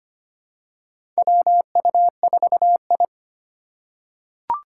Morse code, heard as a keyed tone, about 25 wpm, 700 Hz. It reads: WU4I
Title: USA Call Signs - 3x Character Spacing - 25wpm